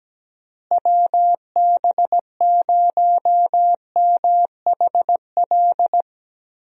Morse code, heard as a keyed tone, about 17 wpm, 700 Hz. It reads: WB0MHL
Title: USA Call Signs - 6 Characters - Repeated 2x - 17wpm